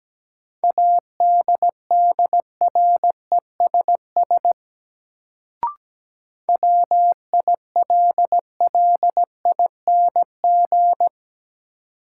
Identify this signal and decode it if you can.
Morse code, heard as a keyed tone, about 17 wpm, 700 Hz. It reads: ADDRESS WILLING
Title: Sentences from Top 1000 Words - New Word Drill - 17wpm